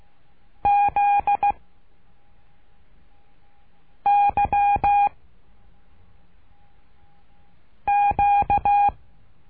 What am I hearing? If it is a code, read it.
Morse code, heard as a keyed tone, about 16 wpm, 810 Hz. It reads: Z Y Q
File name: cursos / CURSO DE TELEGRAFIA / Telegrafia - Trilha 12